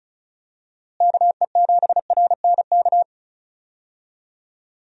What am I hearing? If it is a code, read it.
Morse code, heard as a keyed tone, about 34 wpm, 700 Hz. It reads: KE7RNK
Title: USA Call Signs - 6 Characters - Repeated 2x - 35wpm